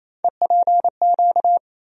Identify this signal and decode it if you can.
Morse code, heard as a keyed tone, about 28 wpm, 700 Hz. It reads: EPQ
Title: USA Call Sign Postfixes - 3 Letters 28wpm